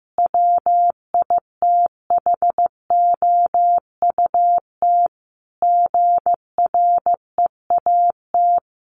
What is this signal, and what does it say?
Morse code, heard as a keyed tone, about 14 wpm, 700 Hz. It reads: WITHOUT GREAT
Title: Sentences from Top 1400 Words 15wpm